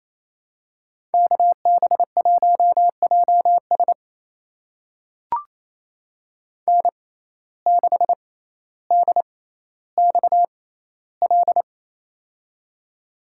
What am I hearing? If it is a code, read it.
Morse code, heard as a keyed tone, about 28 wpm, spaced as for 14 wpm, 700 Hz. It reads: KB1JS N6DXL
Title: USA Call Signs - 5 Characters - 6x Character Spacing - 28wpm